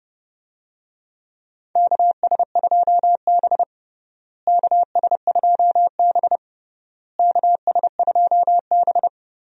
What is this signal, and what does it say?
Morse code, heard as a keyed tone, about 30 wpm, 700 Hz. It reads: KS2B KS2B KS2B
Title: USA Call Signs - 4 Characters - Repeated 3x - 30wpm